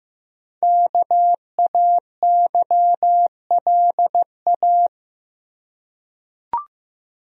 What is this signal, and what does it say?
Morse code, heard as a keyed tone, about 15 wpm, 700 Hz. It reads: KAYLA
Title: USA Common Names - Max 5 Letters - 15wpm